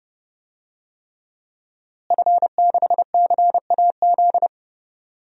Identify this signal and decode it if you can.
Morse code, heard as a keyed tone, about 32 wpm, 700 Hz. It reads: F6CAZ
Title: DXCC-100 Call Signs - 5 Characters - Repeated 3x - 30wpm